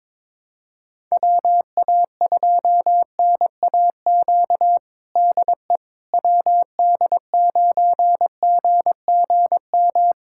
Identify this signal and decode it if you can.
Morse code, heard as a keyed tone, about 22 wpm, 700 Hz. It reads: WA2NAQ DE WD9GGM
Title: USA Callsign DE Callsign - 6 Characters - 22wpm